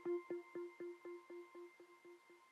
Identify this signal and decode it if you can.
There is no code.